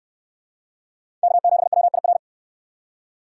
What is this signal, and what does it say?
Morse code, heard as a keyed tone, about 63 wpm, 700 Hz. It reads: K7WIR